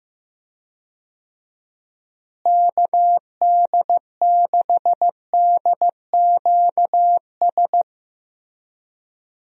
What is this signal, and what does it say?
Morse code, heard as a keyed tone, about 15 wpm, 700 Hz. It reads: KD6DQS
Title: USA Call Signs - 6 Characters - Repeated 2x - 15wpm